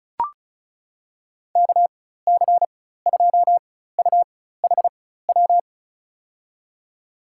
Morse code, heard as a keyed tone, about 36 wpm, spaced as for 16 wpm, 700 Hz. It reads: KC2UHW
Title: USA Call Signs - 6 Characters - 4x Character Spacing - 35wpm